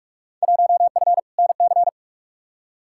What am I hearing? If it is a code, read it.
Morse code, heard as a keyed tone, about 47 wpm, 700 Hz. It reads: 1F NC
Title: Field Day Exchanges - 45wpm